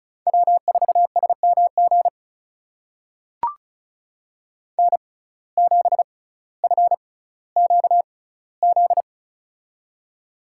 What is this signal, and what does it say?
Morse code, heard as a keyed tone, about 36 wpm, spaced as for 18 wpm, 700 Hz. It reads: W4SMG N7FQZ
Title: USA Call Signs - 5 Characters - 6x Character Spacing - 35wpm